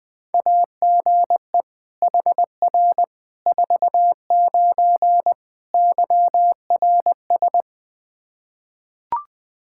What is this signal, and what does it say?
Morse code, heard as a keyed tone, about 20 wpm, 700 Hz. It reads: AGE HR 49 YRS
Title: K1USN SST QSO Practice 20wpm